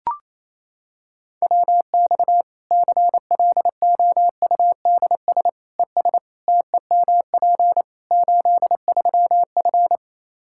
Morse code, heard as a keyed tone, about 28 wpm, 700 Hz. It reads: WX CLOUDS ES TEMP 83F
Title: K1USN SST QSO Practice 28wpm